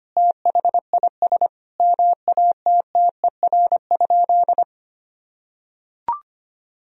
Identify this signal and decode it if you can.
Morse code, heard as a keyed tone, about 25 wpm, 700 Hz. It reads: THIS MATTER?
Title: Sentences from Top 1000 Words - Review Part 1 - 25wpm